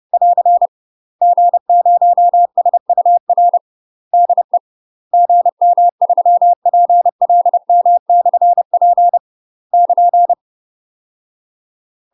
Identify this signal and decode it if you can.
Morse code, heard as a keyed tone, about 30 wpm, 700 Hz. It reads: <AR> G0SUR DE GM3PLM/P <KN>
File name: UK MASTER QSOs TRAINING / 30WPM / UK MASTER QSOs 0051